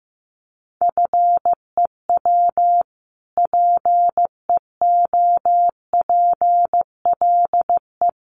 Morse code, heard as a keyed tone, about 15 wpm, 700 Hz. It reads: FEW PEOPLE
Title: Sentences from Top 600 Words 15wpm